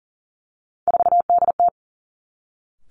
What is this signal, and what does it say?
Morse code, heard as a keyed tone, about 38 wpm, 690 Hz. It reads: 4DT
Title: Three Challenging Letter-Number Combinations 40wpm